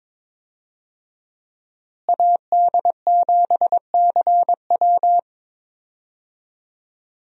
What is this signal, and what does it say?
Morse code, heard as a keyed tone, about 21 wpm, 700 Hz. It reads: AD7CW
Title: USA Call Signs - 5 Characters - Repeated 3x - 22wpm